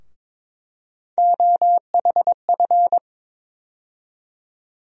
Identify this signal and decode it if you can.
Morse code, heard as a keyed tone, about 22 wpm, 700 Hz. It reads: OHF
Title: USA Call Sign Postfixes - 3 Letters 22wpm